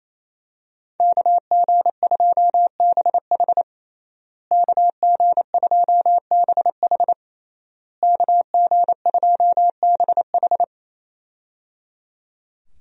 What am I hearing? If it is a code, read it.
Morse code, heard as a keyed tone, about 27 wpm, 700 Hz. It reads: KG2BH KG2BH KG2BH
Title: USA Call Signs - 5 Characters - Repeated 3x - 28wpm